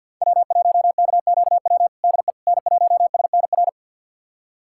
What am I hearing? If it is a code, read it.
Morse code, heard as a keyed tone, about 53 wpm, 700 Hz. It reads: W1KQW DE N1SNR
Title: USA Callsign DE Callsign - 5 Characters - 50wpm